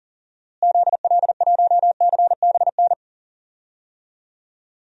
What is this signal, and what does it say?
Morse code, heard as a keyed tone, about 40 wpm, 700 Hz. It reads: ZL1CBN
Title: DXCC-100 Call Signs - 6 Characters - Repeated 3x - 40wpm